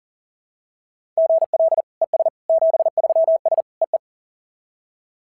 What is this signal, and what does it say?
Morse code, heard as a keyed tone, about 42 wpm, 690 Hz. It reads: GL ES 73S EE
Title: SOTA QSO Practice 40wpm